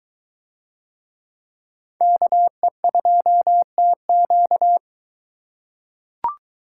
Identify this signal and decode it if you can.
Morse code, heard as a keyed tone, about 23 wpm, 700 Hz. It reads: KE2TQ